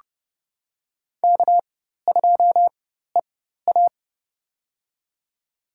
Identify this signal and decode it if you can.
Morse code, heard as a keyed tone, about 30 wpm, spaced as for 14 wpm, 700 Hz. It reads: K2EA
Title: USA Call Signs - 4x Character Spacing - 30wpm